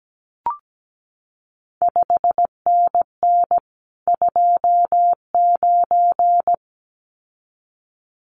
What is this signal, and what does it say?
Morse code, heard as a keyed tone, about 17 wpm, 700 Hz. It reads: 5NN 29
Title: CQ WW DX Contest Exchange Encore - 17wpm